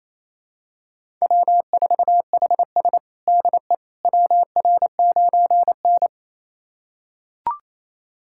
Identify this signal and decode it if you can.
Morse code, heard as a keyed tone, about 28 wpm, 700 Hz. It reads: W4HS DE WR9N
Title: USA Callsign DE Callsign - 4 Characters - 28wpm